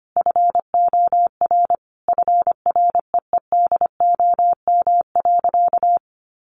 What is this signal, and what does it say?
Morse code, heard as a keyed tone, about 25 wpm, 700 Hz. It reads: FOR FREEDOM.
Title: Sentences from Top 1100 Words 25wpm